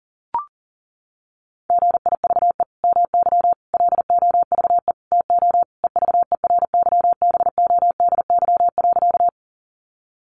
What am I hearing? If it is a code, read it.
Morse code, heard as a keyed tone, about 37 wpm, 700 Hz. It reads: GIVE MY LOVE TO EVERYBODY.
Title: Sentences from Top 1400 Words 40wpm